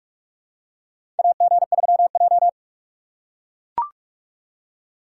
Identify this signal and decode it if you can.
Morse code, heard as a keyed tone, about 42 wpm, 700 Hz. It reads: AG3J
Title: USA Call Signs - 4 Characters - 4x Character Spacing - 45wpm